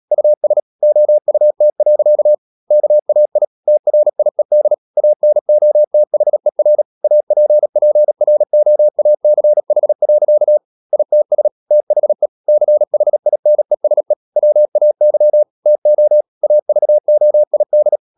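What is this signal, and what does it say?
Morse code, heard as a keyed tone, about 37 wpm, 600 Hz. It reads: US OUT. KAI TRIED ANOTHER APPROACH. ITS THE CHINESE WAY TO AVOID
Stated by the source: Morse code practice at 35–39 wpm sending English text of Ken Follett's novel Never, from CW med Gnister.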